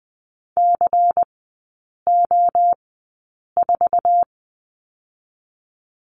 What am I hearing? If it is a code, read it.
Morse code, heard as a keyed tone, about 20 wpm, spaced as for 8 wpm, 700 Hz. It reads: CO4
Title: 166 Lesson 27e - Number 7 - Three Characters 20wpm